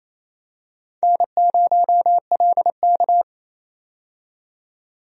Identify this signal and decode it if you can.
Morse code, heard as a keyed tone, about 28 wpm, 700 Hz. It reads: N0LK
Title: USA Call Signs - 4 Characters - 28wpm